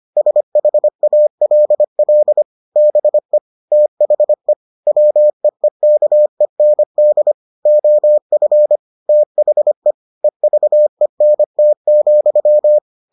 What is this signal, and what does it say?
Morse code, heard as a keyed tone, about 25 wpm, 600 Hz. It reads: SHALL BE THE WEEKEND OF THE EVENT,